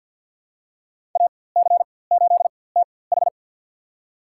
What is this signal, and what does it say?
Morse code, heard as a keyed tone, about 52 wpm, spaced as for 23 wpm, 700 Hz. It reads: AC8TH
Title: USA Call Signs - 5 Characters - 4x Character Spacing - 50wpm